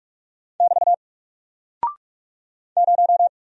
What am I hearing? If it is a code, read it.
Morse code, heard as a keyed tone, about 44 wpm, 700 Hz. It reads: <BT> 0
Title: Single Character (Letter:Number:Punctuation:Prosign) 45wpm